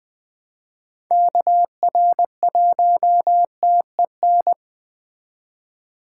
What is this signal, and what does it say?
Morse code, heard as a keyed tone, about 21 wpm, 700 Hz. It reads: KR1TEN